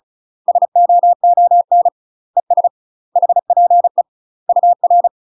Morse code, heard as a keyed tone, about 34 wpm, 700 Hz. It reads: SOON ES HPE UR
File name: UK MASTER QSOs TRAINING / 35WPM / UK MASTER QSOs 0011